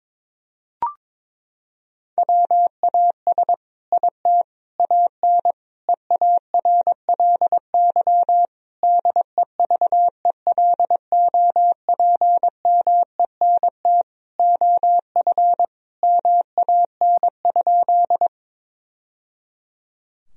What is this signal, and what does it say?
Morse code, heard as a keyed tone, about 22 wpm, 700 Hz. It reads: WAS IT AN EARLY DEVELOPMENT OF MAN?
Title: Sentences from Top 1000 Words - Review Part 4 - 22wpm